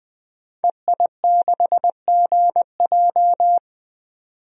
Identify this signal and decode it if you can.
Morse code, heard as a keyed tone, about 21 wpm, 700 Hz. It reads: EI6GJ
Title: DXCC-100 Call Signs - 5 Characters - 20wpm